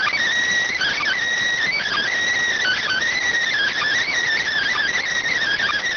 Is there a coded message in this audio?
Yes: Emergency Alert System (SAME) tones from 0.9 to 5.0 s, heard as harsh buzzing bursts.